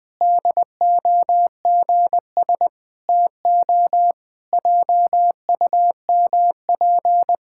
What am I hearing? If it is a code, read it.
Morse code, heard as a keyed tone, about 21 wpm, 700 Hz. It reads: DOGS TO JUMP
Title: Pangram Challenge v2 - 20wpm